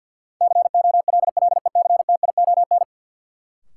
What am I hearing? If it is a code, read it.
Morse code, heard as a keyed tone, about 49 wpm, 700 Hz. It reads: COLLECTION
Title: Sentences from Top 1200 Words - New Word Drill - 50wpm